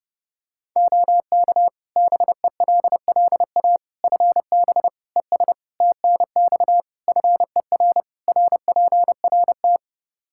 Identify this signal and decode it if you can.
Morse code, heard as a keyed tone, about 30 wpm, 700 Hz. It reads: OK BELLA FB ES TNX FER RPRT
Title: K1USN SST QSO Practice 30wpm